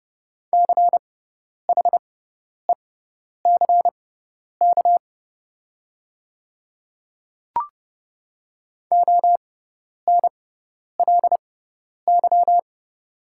Morse code, heard as a keyed tone, about 30 wpm, spaced as for 10 wpm, 700 Hz. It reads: CHECK ONLY
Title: 6x Character Spacing - Top 500 Words - 30wpm